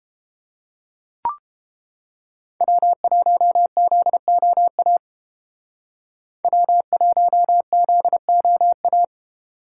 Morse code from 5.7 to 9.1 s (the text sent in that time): W1ZOA